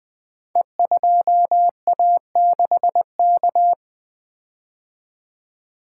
Morse code, heard as a keyed tone, about 20 wpm, 700 Hz. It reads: E2A6K
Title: Code Groups - 5 Letter:Numbers - 20wpm